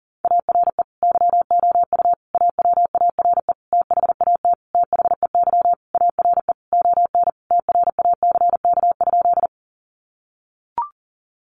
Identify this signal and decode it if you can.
Morse code, heard as a keyed tone, about 40 wpm, 700 Hz. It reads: ARE YOU AWARE THAT THEY ARE ON TRACK?
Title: Sentences from Top 1200 Words 40wpm